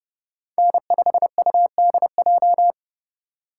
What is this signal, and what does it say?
Morse code, heard as a keyed tone, about 30 wpm, 700 Hz. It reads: N5UDJ
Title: USA Call Signs - 5 Characters - 6x Character Spacing - 30wpm